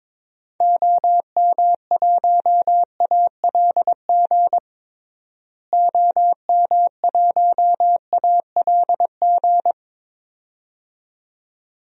Morse code, heard as a keyed tone, about 22 wpm, 700 Hz. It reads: OM1ALG OM1ALG
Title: DXCC-100 Call Signs - 6 Characters - Repeated 2x - 22wpm